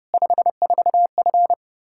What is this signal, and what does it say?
Morse code, heard as a keyed tone, about 30 wpm, 700 Hz. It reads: 54F
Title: Practice QSO - Exchange 2 - Info - 30wpm